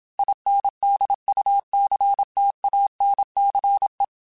Morse code, heard as a keyed tone, about 27 wpm, 800 Hz. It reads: INDUCTANCE
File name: Technical Subjects / Technical Subjects-mp3s / Reactance @25